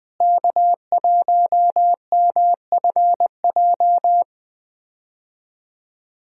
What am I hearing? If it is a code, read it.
Morse code, heard as a keyed tone, about 20 wpm, 700 Hz. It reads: K1MFJ
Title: USA Call Signs - 5 Characters - Repeated 3x - 20wpm